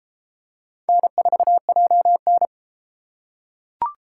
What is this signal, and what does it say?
Morse code, heard as a keyed tone, about 33 wpm, 700 Hz. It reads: N4JN